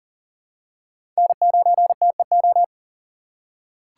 Morse code, heard as a keyed tone, about 40 wpm, 700 Hz. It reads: N9TEO
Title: USA Call Signs - 5 Characters - 3x Character Spacing - 40wpm